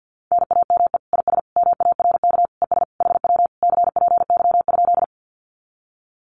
Morse code, heard as a keyed tone, about 50 wpm, 700 Hz. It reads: NAME IS MARK ES HW CPY?
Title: Common QSO Phrases Encore 50wpm